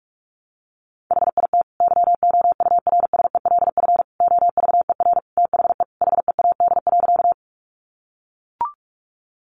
Morse code, heard as a keyed tone, about 44 wpm, 700 Hz. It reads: HIT YOURSELF OVER THE HEAD.
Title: Sentences from Top 1200 Words 45wpm